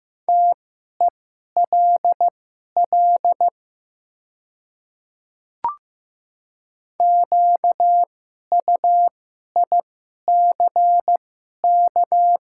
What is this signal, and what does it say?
Morse code, heard as a keyed tone, about 15 wpm, spaced as for 11 wpm, 700 Hz. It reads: TELL QUICK
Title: 2x Character Spacing - Top 500 Words - 15wpm